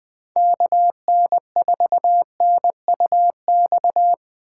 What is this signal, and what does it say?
Morse code, heard as a keyed tone, about 20 wpm, 700 Hz. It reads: KN4NUX
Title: Speed Racing - USA 2x3 Call Signs 17wpm